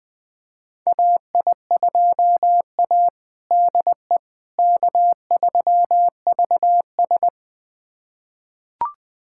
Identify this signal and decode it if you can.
Morse code, heard as a keyed tone, about 21 wpm, 700 Hz. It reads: AI2A DE K3VS
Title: USA Callsign DE Callsign - 4 Characters - 20wpm